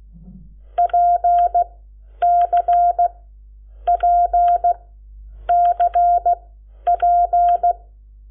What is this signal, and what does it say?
Morse code, heard as a keyed tone, about 16 wpm, spaced as for 9 wpm, 700 Hz. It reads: PCPCP
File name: Lecciones / CLASE Nº7